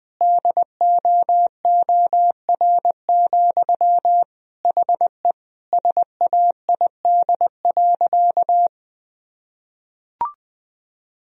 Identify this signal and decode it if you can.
Morse code, heard as a keyed tone, about 21 wpm, 700 Hz. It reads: DOOR, HE SAID.
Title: Sentences from Top 1000 Words - Review Part 2 - 20wpm